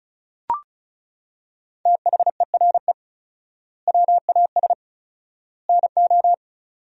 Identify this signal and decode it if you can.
Morse code, heard as a keyed tone, about 36 wpm, 700 Hz. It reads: THERE WAS NO
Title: 4x Word-Spacing - Sentences from Top 400 Words - 35wpm